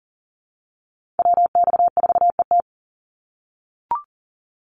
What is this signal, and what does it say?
Morse code, heard as a keyed tone, about 40 wpm, 700 Hz. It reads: WX4ET
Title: NA SOTA Chaser Callsigns 40wpm